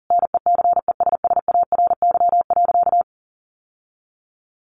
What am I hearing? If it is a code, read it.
Morse code, heard as a keyed tone, about 40 wpm, 700 Hz. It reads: NECESSARY.
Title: Sentences from Top 1100 Words 40wpm